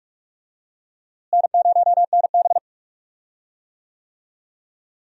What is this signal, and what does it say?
Morse code, heard as a keyed tone, about 46 wpm, 700 Hz. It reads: N0NB
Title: USA Call Signs - 4 Characters - Repeated 2x - 45wpm